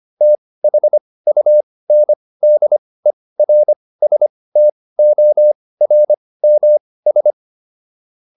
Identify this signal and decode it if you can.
Morse code, heard as a keyed tone, about 25 wpm, spaced as for 18 wpm, 600 Hz. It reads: THUNDERSTORMS